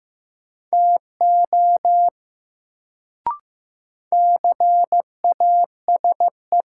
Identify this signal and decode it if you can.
Morse code, heard as a keyed tone, about 15 wpm, 700 Hz. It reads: TO CASE
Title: Sets of 2 Words - New Word Drill - 15wpm